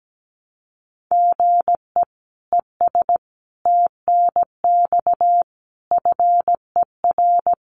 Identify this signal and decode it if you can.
Morse code, heard as a keyed tone, about 17 wpm, 700 Hz. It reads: GE ES TNX FER
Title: Common QSO Phrases Encore 17wpm